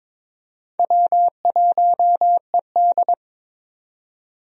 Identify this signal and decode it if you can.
Morse code, heard as a keyed tone, about 22 wpm, 700 Hz. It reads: W1ED
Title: USA Call Signs - 4 Characters - 22wpm